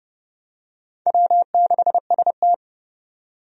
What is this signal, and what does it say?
Morse code, heard as a keyed tone, about 30 wpm, 700 Hz. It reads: W6ST
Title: USA Call Signs - 4 Characters - 30wpm